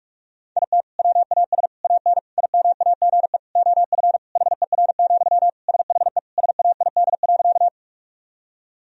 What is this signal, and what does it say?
Morse code, heard as a keyed tone, about 46 wpm, 700 Hz. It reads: IT WAS AN IMAGE OF HER, SHE SAID.
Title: Sentences from Top 1000 Words - Review Part 4 - 45wpm